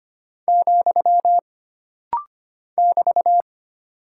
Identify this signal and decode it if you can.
Morse code, heard as a keyed tone, about 25 wpm, 700 Hz. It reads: , <BT>
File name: Single Punctuation-Prosign - Rapid-Fire - 25wpm